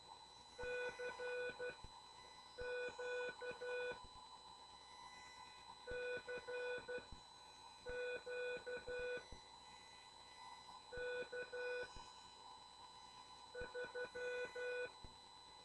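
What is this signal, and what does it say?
Morse code, heard as a keyed tone, about 12 wpm, spaced as for 6 wpm, 490 Hz. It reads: CQ CQ K3
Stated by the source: CW class lesson from The Martian Net with instructor N2KPS.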